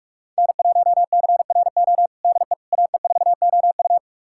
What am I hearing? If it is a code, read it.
Morse code, heard as a keyed tone, about 46 wpm, 700 Hz. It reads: N1CRO DE AE4OU
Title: USA Callsign DE Callsign - 5 Characters - 45wpm